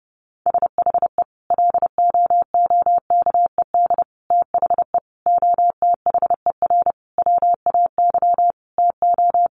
Morse code, heard as a keyed tone, about 30 wpm, 700 Hz. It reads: SHE LOOKED THE OTHER WAY TO
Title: Sentences from Top 800 Words 30wpm